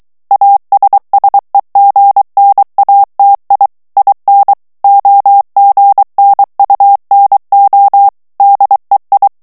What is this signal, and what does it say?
Morse code, heard as a keyed tone, about 23 wpm, 800 Hz. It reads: ASSEGNATI IN OGNUNO DEI
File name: ITALIEN texte clair 2 vitesse 22 mots minute